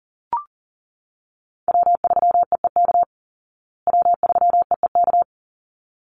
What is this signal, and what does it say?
Morse code, heard as a keyed tone, about 40 wpm, 700 Hz. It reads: W3EEK W3EEK
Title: Call Signs Repeated 3x - 40wpm